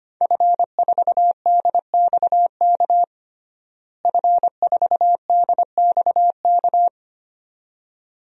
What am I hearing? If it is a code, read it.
Morse code, heard as a keyed tone, about 25 wpm, 700 Hz. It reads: F4DXK F4DXK
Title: DXCC-100 Call Signs - 5 Characters - Repeated 2x - 25wpm